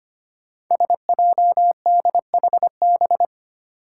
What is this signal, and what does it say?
Morse code, heard as a keyed tone, about 25 wpm, 700 Hz. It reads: SJDHB
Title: Five Challenging Letter-Number Combinations 25wpm